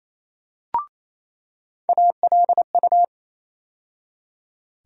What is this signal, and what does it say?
Morse code, heard as a keyed tone, about 28 wpm, 700 Hz. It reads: ALU
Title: USA Call Sign Postfixes - 3 Letters 28wpm